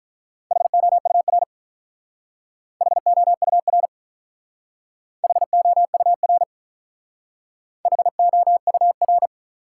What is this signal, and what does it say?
Morse code, heard as a keyed tone, about 45 wpm, 700 Hz. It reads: HOUR HOUR HOUR HOUR